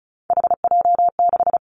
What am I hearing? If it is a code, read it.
Morse code, heard as a keyed tone, about 34 wpm, 700 Hz. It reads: HJ6
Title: Callsign Prefixes for Top 100 DXCC - 35wpm